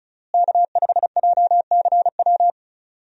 Morse code, heard as a keyed tone, about 36 wpm, 700 Hz. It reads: K5JCW
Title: USA Call Signs - 5 Characters - Repeated 3x - 35wpm